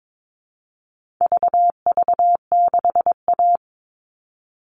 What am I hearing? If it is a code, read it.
Morse code, heard as a keyed tone, about 22 wpm, 700 Hz. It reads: VV6A
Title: All Cut Numbers - 4-Digit 22wpm